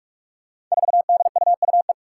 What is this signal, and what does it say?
Morse code, heard as a keyed tone, about 46 wpm, 700 Hz. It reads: 4DUUE
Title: Five Challenging Letter-Number Combinations 45wpm